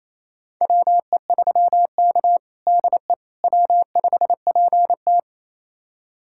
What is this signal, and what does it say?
Morse code, heard as a keyed tone, about 28 wpm, 700 Hz. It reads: WE3K DE W5PT
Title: USA Callsign DE Callsign - 4 Characters - 28wpm